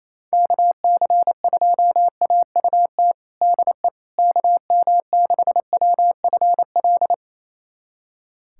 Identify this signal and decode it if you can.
Morse code, heard as a keyed tone, about 27 wpm, 700 Hz. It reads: KC2AUT DE KM6WFL
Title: USA Callsign DE Callsign - 6 Characters - 28wpm